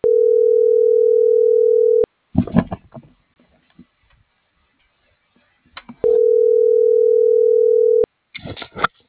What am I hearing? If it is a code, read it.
Ringback tone (North America), 440 + 480 Hz.